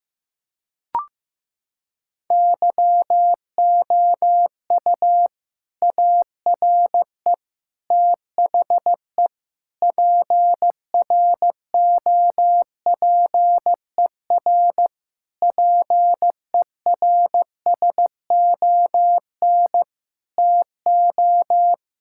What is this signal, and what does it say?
Morse code, heard as a keyed tone, about 15 wpm, 700 Hz. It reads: YOU ARE THE PROPER PERSON TO
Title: Sentences from Top 1000 Words - Review Part 5 - 15wpm